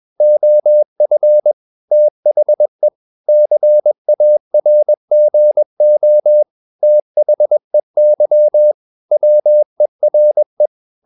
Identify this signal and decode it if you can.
Morse code, heard as a keyed tone, about 21 wpm, 600 Hz. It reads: OF THE CARGO THEY WERE
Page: CW med Gnister